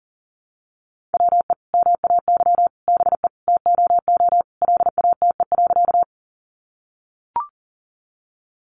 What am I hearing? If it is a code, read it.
Morse code, heard as a keyed tone, about 40 wpm, 700 Hz. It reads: WE MAY BE TOO LATE.
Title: Sentences from Top 300 Words 40wpm